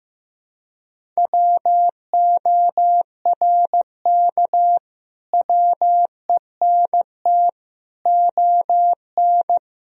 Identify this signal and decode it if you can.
Morse code, heard as a keyed tone, about 16 wpm, 700 Hz. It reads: WORK WENT ON
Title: Sentences from Top 1000 Words - Review Part 1 - 15wpm